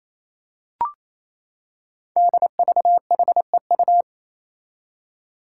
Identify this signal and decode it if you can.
Morse code, heard as a keyed tone, about 28 wpm, 700 Hz. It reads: DVHEU
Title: Five Challenging Letter-Number Combinations 28wpm